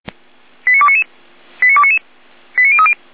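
Five-tone selective call (ZVEI), heard as quick runs of five tones, tones of about 70 ms. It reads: 89299 89299 89030
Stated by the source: ZVEI five-tone selective call received via SDR.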